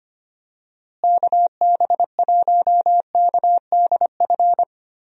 Morse code, heard as a keyed tone, about 25 wpm, 700 Hz. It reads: KB1KDF
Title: USA Call Signs - 6 Characters - Repeated 3x - 25wpm